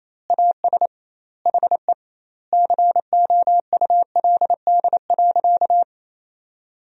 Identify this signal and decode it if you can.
Morse code, heard as a keyed tone, about 28 wpm, 700 Hz. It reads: AS HE COULD.
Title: 2x Word-Spacing - Sentences from Top 500 Words - 28wpm